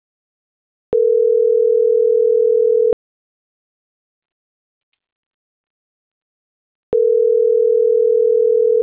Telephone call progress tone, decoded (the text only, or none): ringback tone